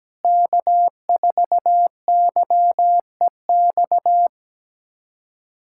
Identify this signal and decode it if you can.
Morse code, heard as a keyed tone, about 17 wpm, 700 Hz. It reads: K4YEX
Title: USA Callsign DE Callsign - 5 Characters - 17wpm